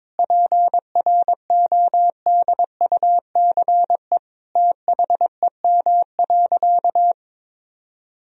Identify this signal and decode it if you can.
Morse code, heard as a keyed tone, about 22 wpm, 700 Hz. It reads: PRODUCE THEM.
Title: Sentences from Top 1000 Words - Review Part 3 - 22wpm